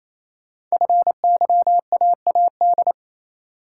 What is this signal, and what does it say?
Morse code, heard as a keyed tone, about 28 wpm, 700 Hz. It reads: FYAAD
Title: Code Groups - 5 Letters - 28wpm